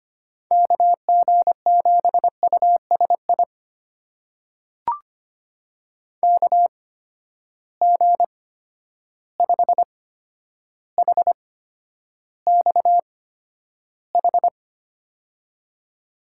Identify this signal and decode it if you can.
Morse code, heard as a keyed tone, about 25 wpm, spaced as for 11 wpm, 700 Hz. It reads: KG7USI KG5HXH
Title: USA Call Signs - 8x Character Spacing - 25wpm